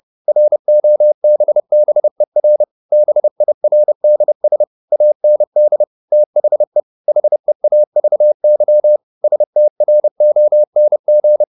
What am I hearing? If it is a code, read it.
Morse code, heard as a keyed tone, about 30 wpm, 600 Hz. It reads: ROBBER BIRDS AND THE HEAVY STRONG
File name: the old man and the sea / the old man and the sea 1 30wpm